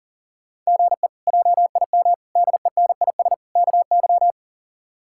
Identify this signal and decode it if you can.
Morse code, heard as a keyed tone, about 40 wpm, 700 Hz. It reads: GE JIM DENIS KY
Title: K1USN SST QSO Practice - 40wpm